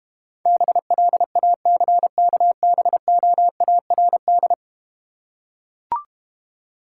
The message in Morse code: BLACKBOARD

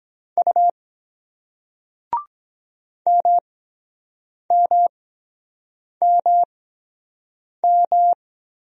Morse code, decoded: U MMMM

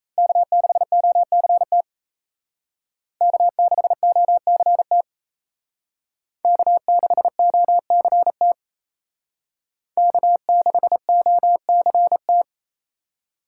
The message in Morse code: K6OCT K6OCT K6OCT K6OCT

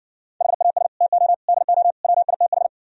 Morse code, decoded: HAS TO DO WITH

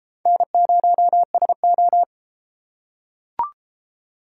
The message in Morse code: N0SO

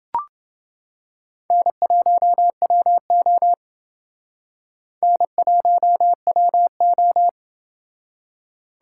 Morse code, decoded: N1WO N1WO

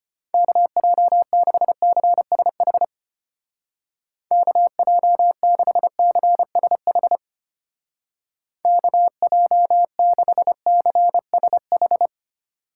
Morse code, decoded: KJ6CSH KJ6CSH KJ6CSH

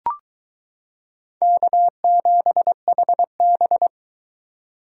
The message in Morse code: K7HB

